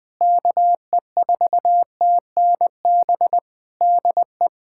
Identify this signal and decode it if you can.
Morse code, heard as a keyed tone, about 21 wpm, 700 Hz. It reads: KE4TNB DE
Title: Practice QSO - Exchange 4 - Ending - 20wpm